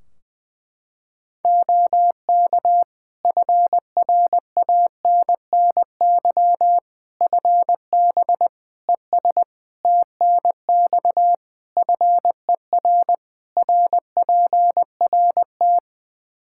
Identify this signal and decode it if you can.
Morse code, heard as a keyed tone, about 21 wpm, 700 Hz. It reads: OK FRANNY FB ES TNX FER RPRT
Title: Practice QSO - Exchange 2 - Info - 20wpm